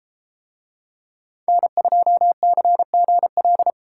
Morse code, heard as a keyed tone, about 33 wpm, 700 Hz. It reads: N2CGL